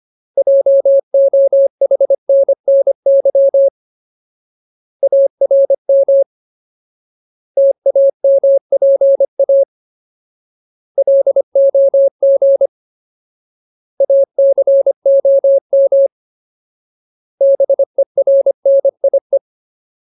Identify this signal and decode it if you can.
Morse code, heard as a keyed tone, about 25 wpm, 550 Hz. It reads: JOHNNY ARM TAMPA LOG ACOM BERNIE
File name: Ultimate Word List / 25WPM Ultimate